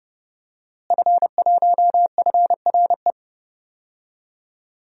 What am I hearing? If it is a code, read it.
Morse code, heard as a keyed tone, about 30 wpm, 700 Hz. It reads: F1FRE
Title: DXCC-100 Call Signs - 5 Characters - Repeated 3x - 30wpm